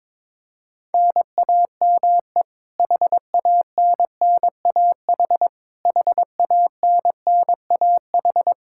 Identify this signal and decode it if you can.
Morse code, heard as a keyed tone, about 21 wpm, 700 Hz. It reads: NAME HANNAH HANNAH
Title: Practice QSO - Exchange 1 - Beginning - 22wpm